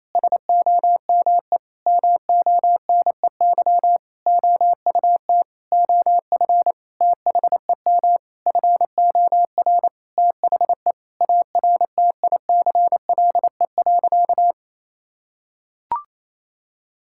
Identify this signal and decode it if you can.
Morse code, heard as a keyed tone, about 28 wpm, 700 Hz. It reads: SOME MONEY OUT OF THEM FOR THE ARTICLE.
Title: Sentences from Top 1000 Words - Review Part 5 - 28wpm